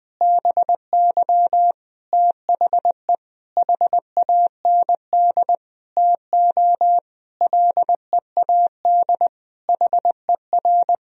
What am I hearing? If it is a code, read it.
Morse code, heard as a keyed tone, about 21 wpm, 700 Hz. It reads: BY THE HAND TO LEAD HER
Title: Sentences from Top 1000 Words - Review Part 2 - 20wpm